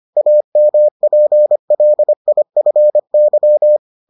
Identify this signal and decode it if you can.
Morse code, heard as a keyed tone, about 25 wpm, 600 Hz. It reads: AMPLIFY